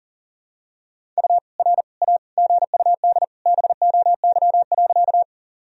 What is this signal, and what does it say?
Morse code, heard as a keyed tone, about 40 wpm, 700 Hz. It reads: U R A GUD BOY.
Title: Sentences from Top 200 Words and Abbreviations 40wpm